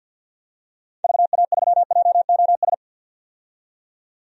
Morse code, heard as a keyed tone, about 50 wpm, 700 Hz. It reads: VA3JOS
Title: DXCC-100 Call Signs - 6 Characters - 50wpm